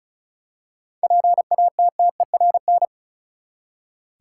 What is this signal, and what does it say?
Morse code, heard as a keyed tone, about 36 wpm, 700 Hz. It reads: PATTERN
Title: Seven Letter Words - 35wpm